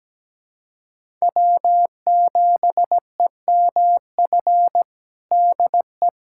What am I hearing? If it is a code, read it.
Morse code, heard as a keyed tone, about 17 wpm, 700 Hz. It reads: W7EMF DE
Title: USA Callsign DE Callsign - 5 Characters - 17wpm